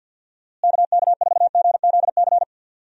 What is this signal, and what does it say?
Morse code, heard as a keyed tone, about 46 wpm, 700 Hz. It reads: KK4GZC
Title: USA Call Signs - 6 Characters - Repeated 2x - 50wpm